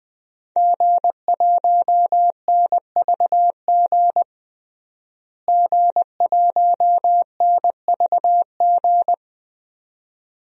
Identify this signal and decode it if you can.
Morse code, heard as a keyed tone, about 20 wpm, 700 Hz. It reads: G1NVG G1NVG
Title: DXCC-100 Call Signs - 5 Characters - Repeated 2x - 20wpm